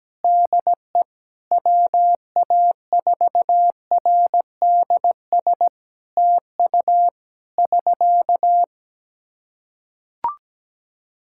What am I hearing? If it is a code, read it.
Morse code, heard as a keyed tone, about 17 wpm, 700 Hz. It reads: DE WA4RDS TU <SK>
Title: Practice QSO - Exchange 4 - Ending - 17wpm